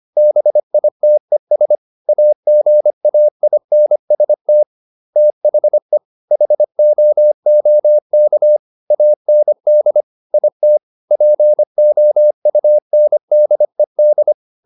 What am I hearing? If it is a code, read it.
Morse code, heard as a keyed tone, about 25 wpm, 600 Hz. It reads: BITES AGAINST THE HOOK AND IT POUNDED